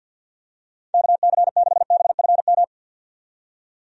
Morse code, heard as a keyed tone, about 51 wpm, 700 Hz. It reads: KC6BFM
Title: USA Call Signs - 6 Characters - 50wpm